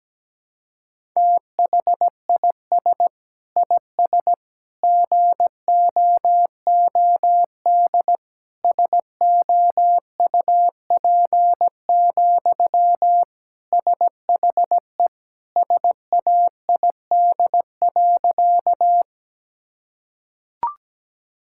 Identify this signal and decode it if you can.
Morse code, heard as a keyed tone, about 17 wpm, 700 Hz. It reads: THIS IS GOOD SOUP, SHE SAID.
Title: Sentences from Top 1000 Words - Review Part 6 - 17wpm